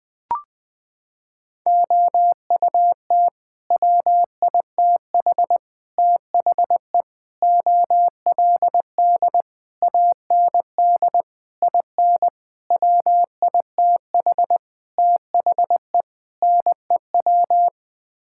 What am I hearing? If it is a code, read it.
Morse code, heard as a keyed tone, about 20 wpm, 700 Hz. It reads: OUT WITH THE OLD AND IN WITH THE NEW
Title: Sets of 9 Words - 20wpm